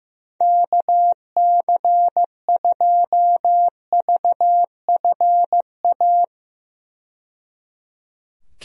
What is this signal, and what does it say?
Morse code, heard as a keyed tone, about 15 wpm, 700 Hz. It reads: KC2VFA
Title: USA Call Signs - 6 Characters - 15wpm